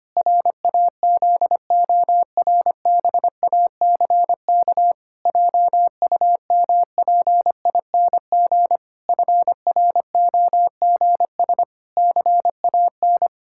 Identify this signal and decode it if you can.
Morse code, heard as a keyed tone, about 25 wpm, 700 Hz. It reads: RAZORBACK JUMPING FROGS CAN
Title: Pangram Challenge v2 - 25wpm